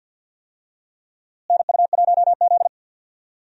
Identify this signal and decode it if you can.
Morse code, heard as a keyed tone, about 52 wpm, 700 Hz. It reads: NU1Z
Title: USA Call Signs - 4 Characters - Repeated 2x - 50wpm